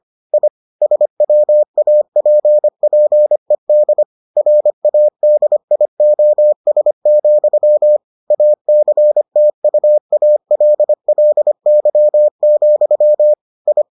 Morse code, heard as a keyed tone, about 25 wpm, 600 Hz. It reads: I SWAPPED RADIOS, ACTUALLY, I